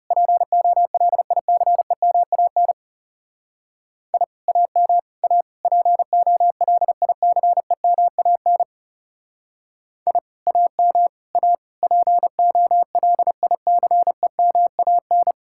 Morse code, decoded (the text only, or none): POLICEMAN I AM A POLICEMAN I AM A POLICEMAN